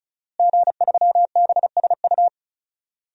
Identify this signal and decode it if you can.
Morse code, heard as a keyed tone, about 35 wpm, 700 Hz. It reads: G3BSU
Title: DXCC-100 Call Signs - 5 Characters - Repeated 3x - 35wpm